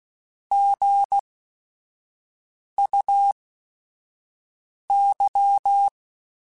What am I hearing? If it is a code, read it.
Morse code, heard as a keyed tone, about 16 wpm, spaced as for 5 wpm, 800 Hz. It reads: GUY
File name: Chuckles / Chuckles-mp3s / HumanErr @5